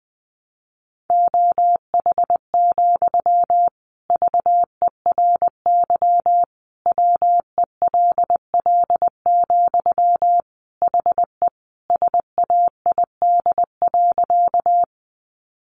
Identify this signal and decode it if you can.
Morse code, heard as a keyed tone, about 20 wpm, 700 Hz. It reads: OH, VERY WELL, HE SAID.
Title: Sentences from Top 500 Words 20wpm